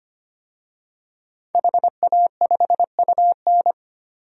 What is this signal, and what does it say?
Morse code, heard as a keyed tone, about 25 wpm, 700 Hz. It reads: HA5UN
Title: DXCC-100 Call Signs - 5 Characters - 25wpm